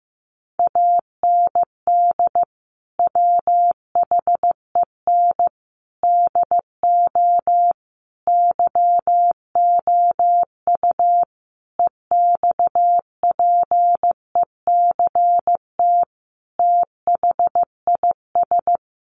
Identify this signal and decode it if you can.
Morse code, heard as a keyed tone, about 15 wpm, 700 Hz. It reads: AND WHEN DO YOU EXPECT THIS
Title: Sentences from Top 1100 Words 15wpm